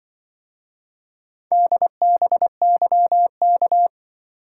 Morse code, heard as a keyed tone, about 24 wpm, 700 Hz. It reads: DBYK
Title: Speed-Racing - Four Letter-Number Combinations - 20wpm